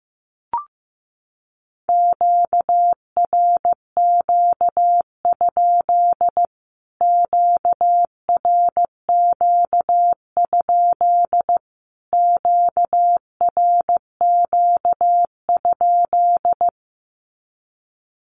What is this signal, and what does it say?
Morse code, heard as a keyed tone, about 15 wpm, 700 Hz. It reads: QRQ? QRQ? QRQ?
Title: Q-Codes Repeated 3x 15wpm